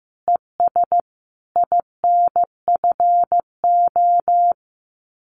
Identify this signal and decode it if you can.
Morse code, heard as a keyed tone, about 14 wpm, 700 Hz. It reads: ES INFO
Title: Common QSO Phrases Encore 15wpm